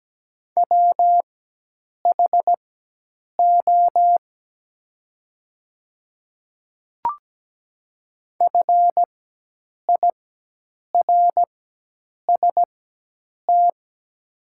Morse code, heard as a keyed tone, about 17 wpm, spaced as for 8 wpm, 700 Hz. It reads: WHO FIRST
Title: 4x Character Spacing - Top 300 Words - 17wpm